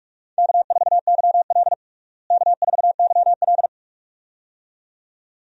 Morse code, heard as a keyed tone, about 44 wpm, 700 Hz. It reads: K4YL K4YL
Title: USA Call Signs - 4 Characters - Repeated 2x - 45wpm